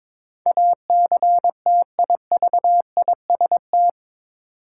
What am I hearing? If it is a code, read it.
Morse code, heard as a keyed tone, about 22 wpm, 700 Hz. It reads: ACTIVIST
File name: practice / 22wpm / Eight Letter Words - 22wpm